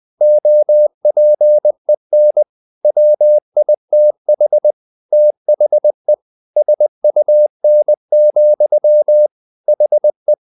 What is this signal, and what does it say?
Morse code, heard as a keyed tone, about 20 wpm, 600 Hz. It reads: OPEN WITH THE SUN, HE
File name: the old man and the sea / the old man and the sea part 1 20wpm